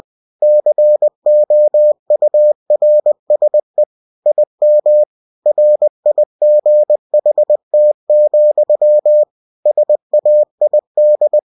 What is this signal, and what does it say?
Morse code, heard as a keyed tone, about 20 wpm, 600 Hz. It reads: COURSE IM RIGHT, SAID
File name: cw-books / winnie-the-pooh / winnie-20wpm-page-0027